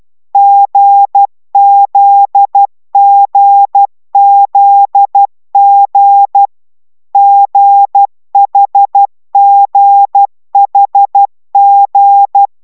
Morse code, heard as a keyed tone, about 12 wpm, 800 Hz. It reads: GZGZG GHGHG